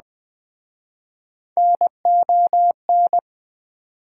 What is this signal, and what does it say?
Morse code, heard as a keyed tone, about 23 wpm, 690 Hz. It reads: NON